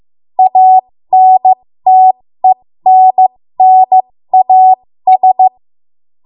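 Morse code, heard as a keyed tone, about 15 wpm, spaced as for 13 wpm, 750 Hz. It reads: ANTENNAS
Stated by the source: 13 wpm Morse code practice archive transmission.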